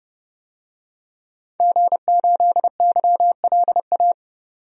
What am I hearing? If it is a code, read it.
Morse code, heard as a keyed tone, about 31 wpm, 700 Hz. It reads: G8YLA
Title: DXCC-100 Call Signs - 5 Characters - 30wpm